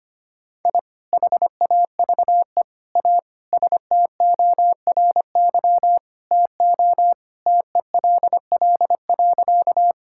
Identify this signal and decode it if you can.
Morse code, heard as a keyed tone, about 25 wpm, 700 Hz. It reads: I HAVE A STORY TO TELL.
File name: practice / 25wpm / Sentences with 2-Word Phrases - 25wpm